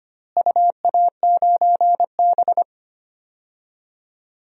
Morse code, heard as a keyed tone, about 25 wpm, 700 Hz. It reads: UA9B
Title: DXCC-100 Call Signs - 4 Characters - Repeated 3x - 25wpm